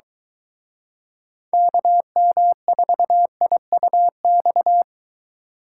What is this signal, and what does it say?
Morse code, heard as a keyed tone, about 23 wpm, 700 Hz. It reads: KM4IUX